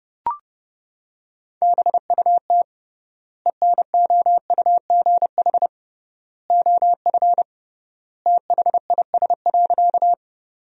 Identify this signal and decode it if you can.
Morse code, heard as a keyed tone, about 32 wpm, 700 Hz. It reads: BUT ENOUGH OF THIS.
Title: 3x Word-Spacing - Sentences from Top 400 Words - 30wpm